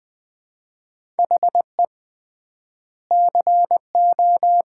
Morse code, heard as a keyed tone, about 20 wpm, 700 Hz. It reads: HE CO
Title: 3x Word-Spacing - Sets of 3 Words - 20wpm